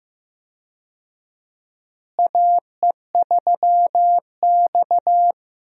Morse code, heard as a keyed tone, about 15 wpm, 700 Hz. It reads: AE3X
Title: USA Call Signs - 4 Characters - Repeated 2x - 15wpm